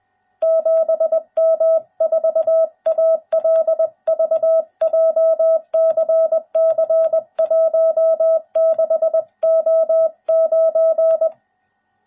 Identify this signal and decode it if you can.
Morse code, heard as a keyed tone, about 20 wpm, 660 Hz. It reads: 7M4ALV JCC 16O9